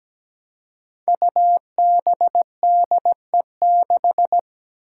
Morse code, heard as a keyed tone, about 17 wpm, 700 Hz. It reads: UBDE6
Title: Five Challenging Letter-Number Combinations 17wpm